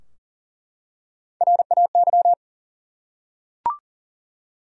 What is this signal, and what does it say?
Morse code, heard as a keyed tone, about 40 wpm, 700 Hz. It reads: RAY
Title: USA Common Names - Max 4 Letters - 40wpm